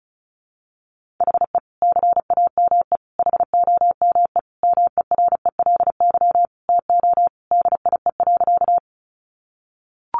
Morse code, heard as a keyed tone, about 34 wpm, 700 Hz. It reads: HE CAME HOME MERELY TO DIE.
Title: Sentences from Top 1400 Words 35wpm